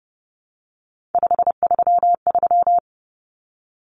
Morse code, heard as a keyed tone, about 29 wpm, 700 Hz. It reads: 533
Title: Numbers - 3-Digit 30wpm